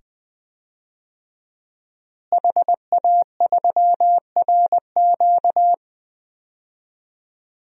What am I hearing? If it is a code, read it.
Morse code, heard as a keyed tone, about 21 wpm, 700 Hz. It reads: HA3RQ
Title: DXCC-100 Call Signs - 5 Characters - Repeated 2x - 20wpm